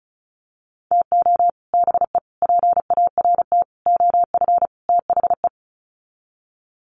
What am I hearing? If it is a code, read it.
Morse code, heard as a keyed tone, about 34 wpm, 700 Hz. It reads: TO BE PART OF THE
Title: Sets of 5 Words 35wpm